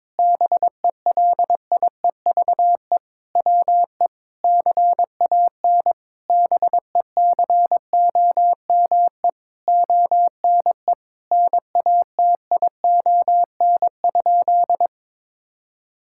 Morse code, heard as a keyed tone, about 22 wpm, 700 Hz. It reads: BELIEVE WE CAN BECOME ONE NATION?
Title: Sentences from Top 1000 Words - Review Part 6 - 22wpm